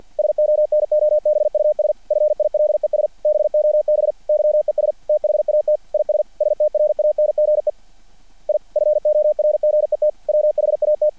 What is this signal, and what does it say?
Morse code, heard as a keyed tone, about 50 wpm, 600 Hz. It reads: SONOBUS FILES BOB YES THAT IS STRANGE I FORGET WHAT
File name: archived / CRSnet